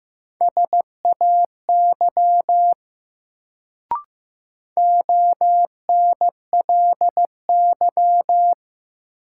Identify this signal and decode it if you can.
Morse code, heard as a keyed tone, about 15 wpm, 700 Hz. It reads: SAY ONLY
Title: Sentences from Top 200 Words - New Word Drill - 15wpm